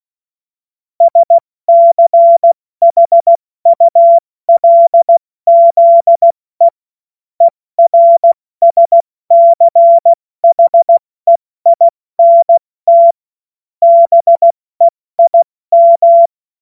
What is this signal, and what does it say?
Morse code, heard as a keyed tone, about 16 wpm, spaced as for 14 wpm, 680 Hz. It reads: SCHULZE ERSCHEINT BEIM
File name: Morsekurs / Witze / Witz 19 März 2024 70 F80